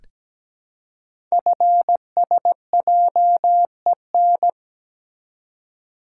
Morse code, heard as a keyed tone, about 17 wpm, 700 Hz. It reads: FSJEN
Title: Code Groups - 5 Letters - 17wpm